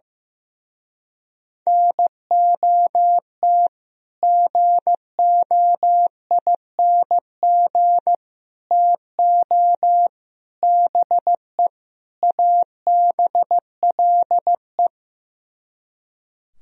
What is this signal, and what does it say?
Morse code, heard as a keyed tone, about 15 wpm, 700 Hz. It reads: NOT GOING TO BE ABLE
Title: Speed-Racing - Sets of 5 Words 15wpm